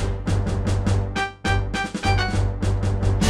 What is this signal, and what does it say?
There is no code.